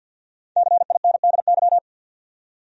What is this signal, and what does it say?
Morse code, heard as a keyed tone, about 47 wpm, 700 Hz. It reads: CINDY
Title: USA Common Names - Max 5 Letters - 50wpm